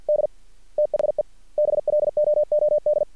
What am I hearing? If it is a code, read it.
Morse code, heard as a keyed tone, about 50 wpm, 600 Hz. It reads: D THE BLOOD
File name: archived / CRSnet